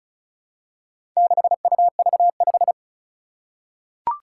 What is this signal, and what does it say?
Morse code, heard as a keyed tone, about 36 wpm, 700 Hz. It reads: 6UV5
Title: Four Challenging Letter-Number Combinations 35wpm